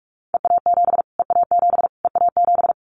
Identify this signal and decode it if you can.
Morse code, heard as a keyed tone, about 44 wpm, 700 Hz. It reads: EA7 EA7 EA7
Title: Callsign Prefixes for Top 25 DXCC - Repeated 3x - 45wpm